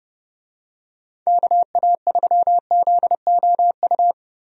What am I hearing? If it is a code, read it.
Morse code, heard as a keyed tone, about 31 wpm, 700 Hz. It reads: KA3ZOU
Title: USA Call Signs - 6 Characters - 30wpm